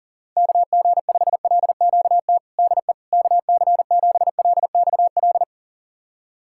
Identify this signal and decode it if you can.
Morse code, heard as a keyed tone, about 40 wpm, 700 Hz. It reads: KG5LQT DE KC7LXL
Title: USA Callsign DE Callsign - 6 Characters - 40wpm